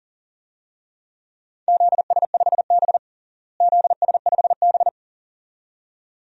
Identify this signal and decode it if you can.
Morse code, heard as a keyed tone, about 37 wpm, 700 Hz. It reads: ZS5B ZS5B
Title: DXCC-100 Call Signs - 4 Characters - Repeated 2x - 40wpm